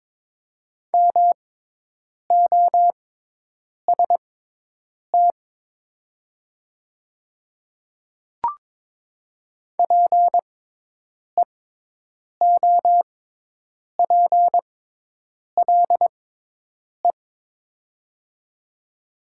Morse code, heard as a keyed tone, about 22 wpm, spaced as for 8 wpm, 700 Hz. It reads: MOST PEOPLE
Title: 6x Character Spacing - Top 100 Words - 22wpm